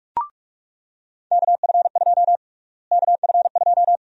Morse code, heard as a keyed tone, about 46 wpm, 700 Hz. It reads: KF2 KF2
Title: USA Call Sign Prefixes - Repeated 2x - 45wpm